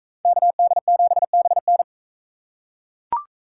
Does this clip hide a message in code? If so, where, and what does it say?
Yes: Morse code from 0.3 to 1.8 s, heard as a keyed tone, about 41 wpm, 700 Hz. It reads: KD7BN